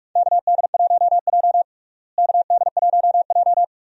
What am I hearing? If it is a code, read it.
Morse code, heard as a keyed tone, about 46 wpm, 700 Hz. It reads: KD1J KD1J
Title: USA Call Signs - 4 Characters - Repeated 2x - 45wpm